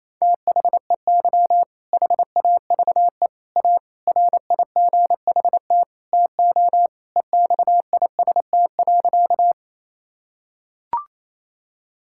Morse code, heard as a keyed tone, about 28 wpm, 700 Hz. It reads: THEY HAVE A RIGHT TO EXIST.
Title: Sentences from Top 1000 Words - Review Part 5 - 28wpm